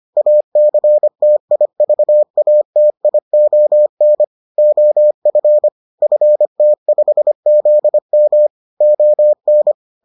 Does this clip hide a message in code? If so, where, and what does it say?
Yes: Morse code from 0.2 to 9.7 s, heard as a keyed tone, about 25 wpm, 600 Hz. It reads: ACTIVATION OF FT5ZM ON